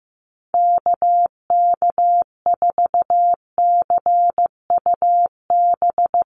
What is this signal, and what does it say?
Morse code, heard as a keyed tone, about 15 wpm, 700 Hz. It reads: KK4CUB
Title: USA Call Signs 15wpm